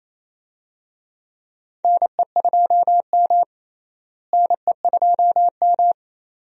Morse code, heard as a keyed tone, about 28 wpm, 700 Hz. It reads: NE2M NE2M
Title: USA Call Signs - 4 Characters - Repeated 2x - 28wpm